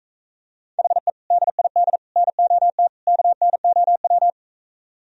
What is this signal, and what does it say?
Morse code, heard as a keyed tone, about 44 wpm, 700 Hz. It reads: HE DID NOT KNOW